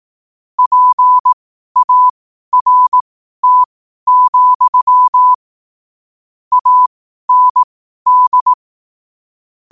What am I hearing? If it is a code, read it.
Morse code, heard as a keyed tone, about 18 wpm, spaced as for 12 wpm, 1000 Hz. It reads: PART, AND
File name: qotdfiles / 12wpm